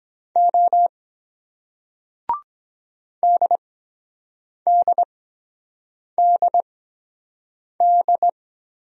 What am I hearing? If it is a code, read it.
Morse code, heard as a keyed tone, about 23 wpm, spaced as for 7 wpm, 700 Hz. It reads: O DDDD